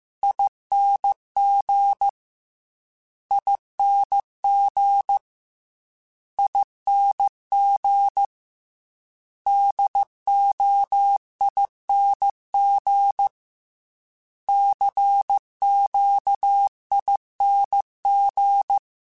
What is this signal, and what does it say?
Morse code, heard as a keyed tone, about 15 wpm, 800 Hz. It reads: ING ING ING DOING CQING